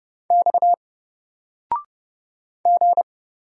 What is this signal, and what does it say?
Morse code, heard as a keyed tone, about 30 wpm, 700 Hz. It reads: X G
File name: Single-Letter-Number-30wpm